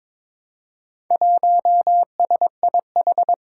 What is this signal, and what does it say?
Morse code, heard as a keyed tone, about 22 wpm, 700 Hz. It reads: 1SIH
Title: Four Challenging Letter-Number Combinations 22wpm